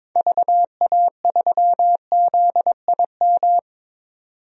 Morse code, heard as a keyed tone, about 22 wpm, 700 Hz. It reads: VA3ZIM
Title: DXCC-100 Call Signs - 6 Characters - 22wpm